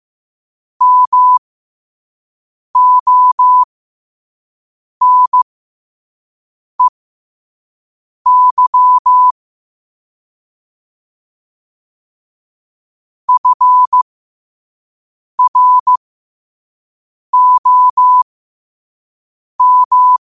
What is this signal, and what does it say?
Morse code, heard as a keyed tone, about 15 wpm, spaced as for 5 wpm, 1000 Hz. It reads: MONEY FROM
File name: qotdfiles / 5wpm